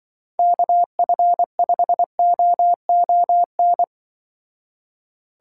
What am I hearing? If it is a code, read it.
Morse code, heard as a keyed tone, about 24 wpm, 700 Hz. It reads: KF5OON
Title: Speed Racing - USA 2x3 Call Signs 20wpm